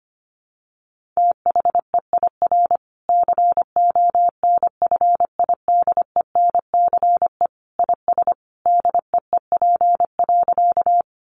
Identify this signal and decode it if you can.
Morse code, heard as a keyed tone, about 25 wpm, 700 Hz. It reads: THEIR CONFIDENCE IS DEEP.
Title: Sentences from Top 1500 Words 25wpm